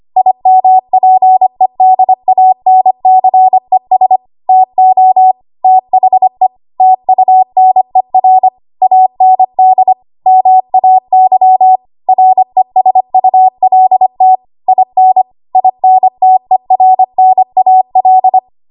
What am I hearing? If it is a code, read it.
Morse code, heard as a keyed tone, about 25 wpm, 750 Hz. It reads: IMPEDANCES TO THE TUNER AND MAY RESULT IN INTERNAL
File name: Archive / 25 WPM / 25WPM